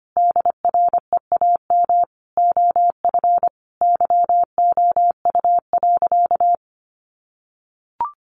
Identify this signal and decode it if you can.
Morse code, heard as a keyed tone, about 25 wpm, 700 Hz. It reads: DREAM OF YOU.
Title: Sentences from Top 1500 Words 25wpm